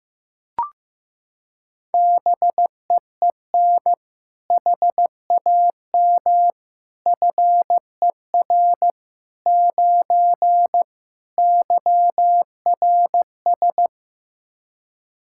Morse code, heard as a keyed tone, about 15 wpm, 700 Hz. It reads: BEEN HAM FER 9 YRS
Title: Practice QSO - Exchange 3 - More Info - 15wpm